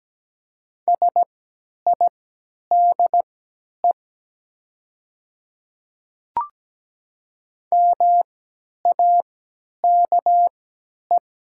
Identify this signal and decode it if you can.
Morse code, heard as a keyed tone, about 17 wpm, spaced as for 10 wpm, 700 Hz. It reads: SIDE MAKE
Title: 3x Character Spacing - Top 200 Words - 17wpm